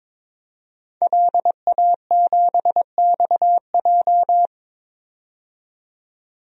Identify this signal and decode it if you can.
Morse code, heard as a keyed tone, about 22 wpm, 700 Hz. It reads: LA7XJ
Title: DXCC-100 Call Signs - 5 Characters - 22wpm